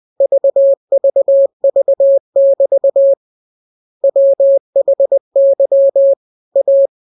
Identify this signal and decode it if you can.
Morse code, heard as a keyed tone, about 20 wpm, 550 Hz. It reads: VVV<BT> WHY A